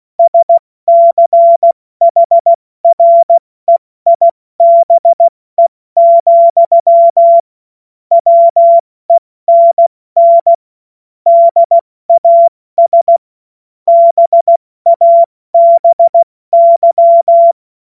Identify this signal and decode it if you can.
Morse code, heard as a keyed tone, about 16 wpm, 680 Hz. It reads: SCHREIBE, WENN DAS BABY
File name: Morsekurs / Witze / Witz 27 Mai 2025 70 F80